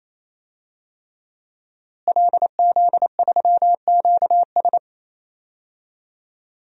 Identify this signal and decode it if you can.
Morse code, heard as a keyed tone, about 28 wpm, 700 Hz. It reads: LZ3QS
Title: DXCC-100 Call Signs - 5 Characters - Repeated 3x - 28wpm